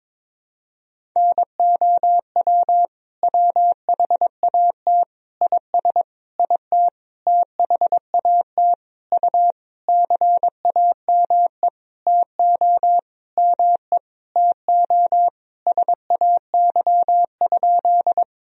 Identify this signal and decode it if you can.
Morse code, heard as a keyed tone, about 22 wpm, 700 Hz. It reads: NOW WHAT IS IT THAT U CAME TO ME TO SAY?
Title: Sentences from Top 200 Words and Abbreviations 22wpm